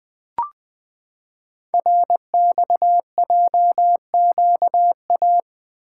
Morse code, heard as a keyed tone, about 20 wpm, 700 Hz. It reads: RXJQA
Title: Code Groups - 5 Letters - 20wpm